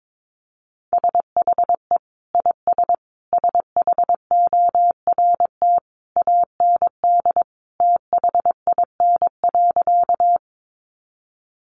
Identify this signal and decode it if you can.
Morse code, heard as a keyed tone, about 22 wpm, 700 Hz. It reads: SHE IS SHORT AND THIN.
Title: Sentences from Top 1200 Words 22wpm